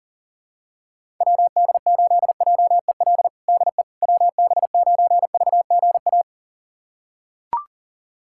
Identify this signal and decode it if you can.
Morse code, heard as a keyed tone, about 40 wpm, 700 Hz. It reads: WD8JEL DE WB9VGA
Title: USA Callsign DE Callsign - 6 Characters - 40wpm